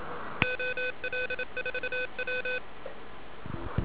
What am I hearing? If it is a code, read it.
Morse code, heard as a keyed tone, about 27 wpm, 3500 Hz, with noise. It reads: OL4W